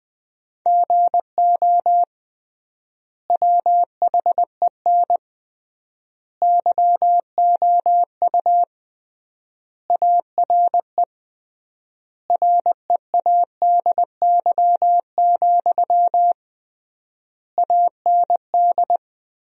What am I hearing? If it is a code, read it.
Morse code, heard as a keyed tone, about 20 wpm, 700 Hz. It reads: GO WHEN YOU ARE READY, AND
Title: 3x Word-Spacing - Sentences from Top 500 Words - 20wpm